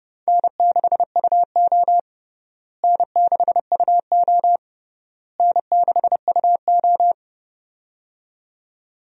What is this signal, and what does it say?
Morse code, heard as a keyed tone, about 30 wpm, 700 Hz. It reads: N6UO N6UO N6UO
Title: USA Call Signs - 4 Characters - Repeated 3x - 30wpm